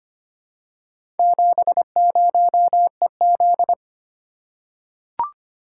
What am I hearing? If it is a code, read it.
Morse code, heard as a keyed tone, about 25 wpm, 690 Hz. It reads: 70EZ
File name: practice / 25wpm / Four Letter-Number Combinations - 25wpm